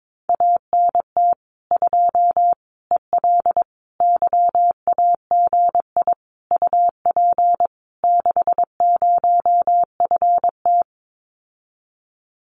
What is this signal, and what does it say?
Morse code, heard as a keyed tone, about 22 wpm, 700 Hz. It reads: ANT 2 EL YAGI UP 60FT
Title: Common QSO Phrases Encore 22wpm